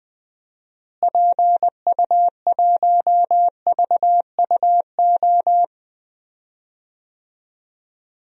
Morse code, heard as a keyed tone, about 21 wpm, 700 Hz. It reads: PU1VUO
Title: DXCC-100 Call Signs - 6 Characters - Repeated 3x - 20wpm